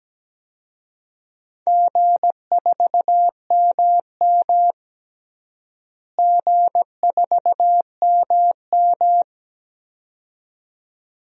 Morse code, heard as a keyed tone, about 17 wpm, 700 Hz. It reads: G4MM G4MM
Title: DXCC-100 Call Signs - 4 Characters - Repeated 2x - 17wpm